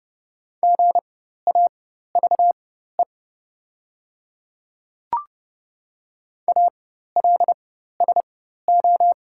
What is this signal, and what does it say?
Morse code, heard as a keyed tone, about 31 wpm, spaced as for 14 wpm, 700 Hz. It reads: GAVE ALSO
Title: 4x Character Spacing - Top 500 Words - 30wpm